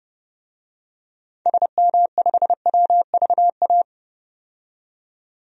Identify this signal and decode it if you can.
Morse code, heard as a keyed tone, about 32 wpm, 700 Hz. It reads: SM5WVA
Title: DXCC-100 Call Signs - 6 Characters - 30wpm